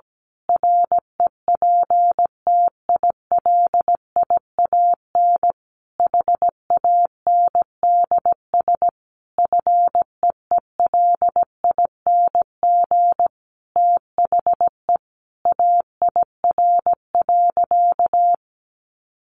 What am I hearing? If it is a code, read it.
Morse code, heard as a keyed tone, about 17 wpm, 700 Hz. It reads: REPTILIAN HANDS FEELING THE AIR.
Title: A Sound Of Thunder 17wpm